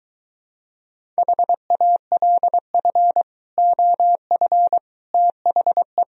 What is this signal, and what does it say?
Morse code, heard as a keyed tone, about 23 wpm, 700 Hz. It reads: HALF OF THE